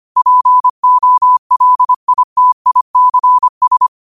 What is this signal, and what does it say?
Morse code, heard as a keyed tone, about 25 wpm, 1000 Hz. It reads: POLITICS